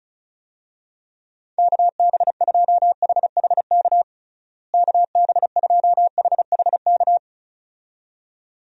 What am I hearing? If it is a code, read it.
Morse code, heard as a keyed tone, about 34 wpm, 700 Hz. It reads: KB2HHK KB2HHK
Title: USA Call Signs - 6 Characters - Repeated 2x - 35wpm